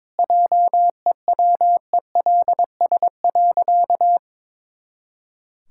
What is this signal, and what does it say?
Morse code, heard as a keyed tone, about 22 wpm, 700 Hz. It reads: JEWELS.
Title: Pangram Challenge v2 - 22wpm